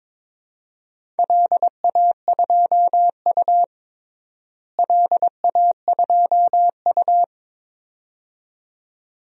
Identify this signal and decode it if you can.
Morse code, heard as a keyed tone, about 22 wpm, 700 Hz. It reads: LA2U LA2U
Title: DXCC-100 Call Signs - 4 Characters - Repeated 2x - 22wpm